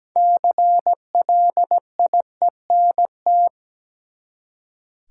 Morse code, heard as a keyed tone, about 17 wpm, 700 Hz. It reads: CLIENT
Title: Six Letter Words - 17wpm